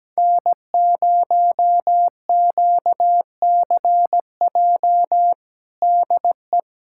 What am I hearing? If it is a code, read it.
Morse code, heard as a keyed tone, about 17 wpm, 700 Hz. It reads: N0QCJ DE
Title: Practice QSO - Exchange 4 - Ending - 17wpm